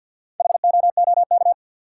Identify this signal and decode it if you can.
Morse code, heard as a keyed tone, about 52 wpm, 700 Hz. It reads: HOOK
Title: Four Letter Words - 50wpm